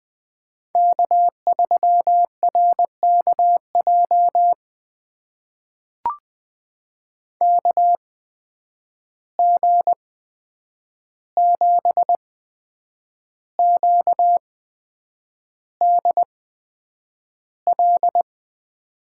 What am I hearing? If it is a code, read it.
Morse code, heard as a keyed tone, about 20 wpm, spaced as for 8 wpm, 700 Hz. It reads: K3RKJ KG7QDL
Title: USA Call Signs - 8x Character Spacing - 20wpm